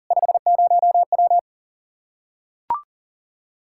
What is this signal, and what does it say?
Morse code, heard as a keyed tone, about 40 wpm, 700 Hz. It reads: 50W
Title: Practice QSO - Exchange 2 - Info - 40wpm